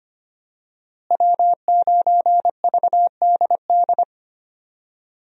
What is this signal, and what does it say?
Morse code, heard as a keyed tone, about 25 wpm, 700 Hz. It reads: W9VDD
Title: USA Call Signs - 5 Characters - Repeated 2x - 25wpm